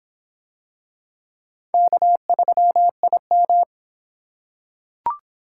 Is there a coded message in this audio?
Yes: Morse code from 1.7 to 3.6 s, heard as a keyed tone, about 24 wpm, 700 Hz. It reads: K3IM